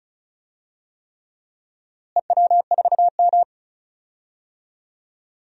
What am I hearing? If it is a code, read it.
Morse code, heard as a keyed tone, about 36 wpm, 700 Hz. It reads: EW4M
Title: DXCC-100 Call Signs - 4 Characters - Repeated 3x - 35wpm